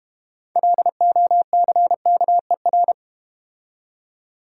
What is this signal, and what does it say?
Morse code, heard as a keyed tone, about 32 wpm, 700 Hz. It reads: LOCKER